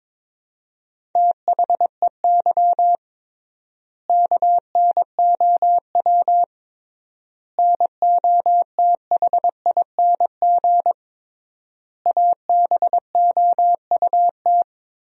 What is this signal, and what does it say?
Morse code, heard as a keyed tone, about 22 wpm, 700 Hz. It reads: THEY KNOW NOTHING ABOUT
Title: 3x Word-Spacing - Sentences from Top 500 Words - 22wpm